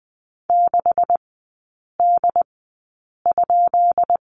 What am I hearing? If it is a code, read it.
Morse code, heard as a keyed tone, about 20 wpm, spaced as for 8 wpm, 700 Hz. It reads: 6D?
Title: 133 Lesson 22e - Question Mark - Three Characters 20wpm